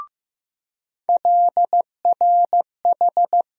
Morse code, heard as a keyed tone, about 15 wpm, 700 Hz. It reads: LRH
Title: USA Call Sign Postfixes - 3 Letters 15wpm